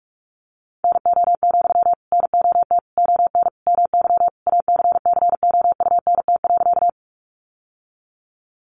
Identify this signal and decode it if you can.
Morse code, heard as a keyed tone, about 44 wpm, 700 Hz. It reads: NO, NOT ON MY ACCOUNT.
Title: Sentences from Top 900 Words 45wpm